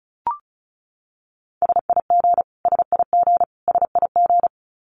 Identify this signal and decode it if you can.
Morse code, heard as a keyed tone, about 34 wpm, 700 Hz. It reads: SIG SIG SIG
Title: Abbreviations Repeated 3x - 35wpm